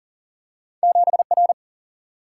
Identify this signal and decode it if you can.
Morse code, heard as a keyed tone, about 40 wpm, 700 Hz. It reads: 7 R
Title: Single-Letter Single-Number 40wpm